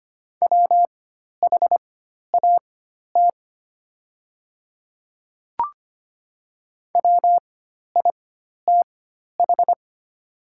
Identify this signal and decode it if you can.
Morse code, heard as a keyed tone, about 25 wpm, spaced as for 12 wpm, 700 Hz. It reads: WHAT WITH
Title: 4x Character Spacing - Top 100 Words - 25wpm